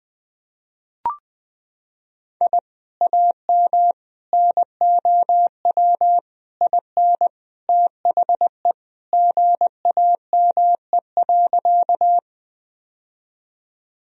Morse code, heard as a keyed tone, about 20 wpm, 700 Hz. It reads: I AM NOW IN THE GAME.
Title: Sentences from Top 1000 Words - Review Part 3 - 20wpm